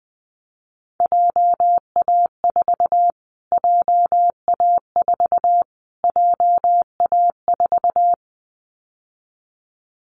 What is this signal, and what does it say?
Morse code, heard as a keyed tone, about 20 wpm, 700 Hz. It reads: JA4 JA4 JA4
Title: Callsign Prefixes for Top 25 DXCC - Repeated 3x - 20wpm